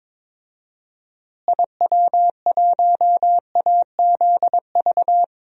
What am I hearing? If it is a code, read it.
Morse code, heard as a keyed tone, about 22 wpm, 700 Hz. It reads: IW1AZV
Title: DXCC-100 Call Signs - 6 Characters - Repeated 3x - 22wpm